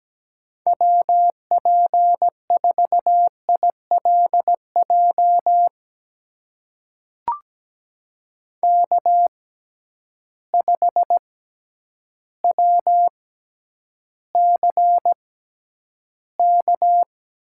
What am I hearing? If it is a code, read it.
Morse code, heard as a keyed tone, about 17 wpm, spaced as for 9 wpm, 700 Hz. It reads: WP4ILJ K5WCK
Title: USA Call Signs - 6x Character Spacing - 17wpm